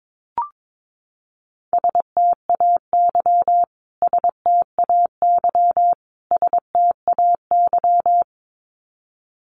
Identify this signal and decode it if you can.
Morse code, heard as a keyed tone, about 22 wpm, 700 Hz. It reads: STAY STAY STAY
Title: Top 500 Words Repeated 22wpm